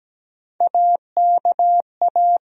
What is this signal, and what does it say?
Morse code, heard as a keyed tone, about 17 wpm, 700 Hz. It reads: AKA
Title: USA Call Sign Postfixes - 3 Letters 17wpm